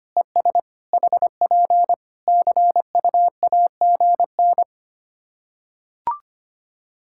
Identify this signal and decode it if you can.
Morse code, heard as a keyed tone, about 25 wpm, 700 Hz. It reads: ES HP CUAGN
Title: Practice QSO - Exchange 4 - Ending - 25wpm